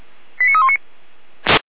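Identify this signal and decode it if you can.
Five-tone selective call (ZVEI), heard as quick runs of five tones, tones of about 70 ms. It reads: 89319
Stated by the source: ZVEI five-tone selective call 89327 received with an SDR.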